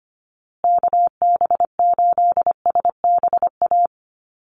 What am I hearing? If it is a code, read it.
Morse code, heard as a keyed tone, about 25 wpm, 700 Hz. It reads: KB8SBA
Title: USA Call Signs Repeated 3x 25wpm